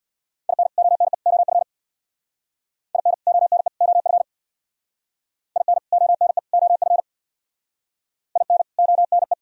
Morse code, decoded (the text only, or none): IN ONE OF IN ONE OF IN ONE OF IN ONE